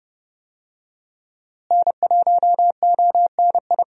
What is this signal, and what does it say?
Morse code, heard as a keyed tone, about 30 wpm, 700 Hz. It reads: N1ONI
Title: USA Call Signs - 5 Characters - Repeated 3x - 30wpm